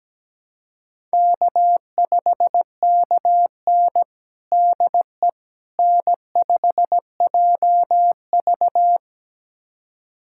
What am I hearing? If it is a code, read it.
Morse code, heard as a keyed tone, about 17 wpm, 700 Hz. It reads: K5KN DE N5JV
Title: USA Callsign DE Callsign - 4 Characters - 17wpm